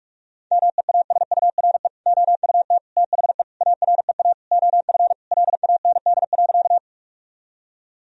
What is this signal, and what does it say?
Morse code, heard as a keyed tone, about 41 wpm, 700 Hz. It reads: MEASURE OUT THE AREA OF LAND.
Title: Sentences from Top 1000 Words - Review Part 2 - 45wpm